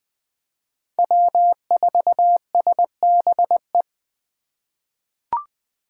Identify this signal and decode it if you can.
Morse code, heard as a keyed tone, about 21 wpm, 700 Hz. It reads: W4SBE
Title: USA Call Signs - 5 Characters - 2x Character Spacing - 20wpm